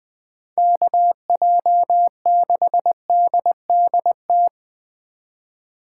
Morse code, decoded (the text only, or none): KJ6DDT